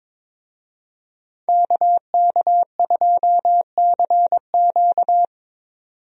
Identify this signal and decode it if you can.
Morse code, heard as a keyed tone, about 22 wpm, 700 Hz. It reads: KK2CQ
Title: USA Call Signs - 5 Characters - 22wpm